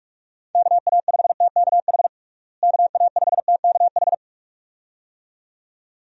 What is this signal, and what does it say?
Morse code, heard as a keyed tone, about 46 wpm, 700 Hz. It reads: KA5TKH KA5TKH
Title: USA Call Signs - 6 Characters - Repeated 2x - 45wpm